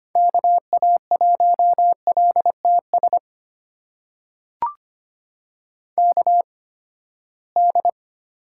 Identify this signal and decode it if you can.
Morse code, heard as a keyed tone, about 25 wpm, spaced as for 17 wpm, 700 Hz. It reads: KA1LTS KD
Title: USA Call Signs - 6 Characters - 8x Character Spacing - 25wpm